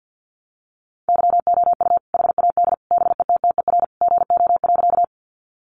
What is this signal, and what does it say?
Morse code, heard as a keyed tone, about 49 wpm, 700 Hz. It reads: YOU HAD BETTER GO.
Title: Sentences from Top 400 Words 50wpm